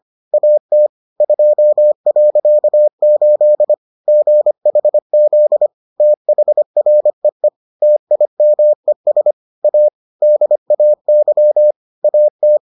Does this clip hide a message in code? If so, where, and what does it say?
Yes: Morse code from 0.3 to 12.6 s, heard as a keyed tone, about 25 wpm, 600 Hz. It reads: AT 2.8 GHZ THREE TIMES A DAY AT